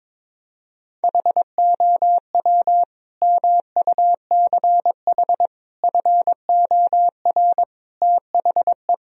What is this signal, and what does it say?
Morse code, heard as a keyed tone, about 22 wpm, 700 Hz. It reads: HOW MUCH FOR THE
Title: Sentences from Top 1000 Words - Review Part 3 - 22wpm